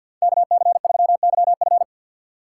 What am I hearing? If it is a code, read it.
Morse code, heard as a keyed tone, about 48 wpm, 700 Hz. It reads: KC3YF
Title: USA Call Signs - 3x Character Spacing - 50wpm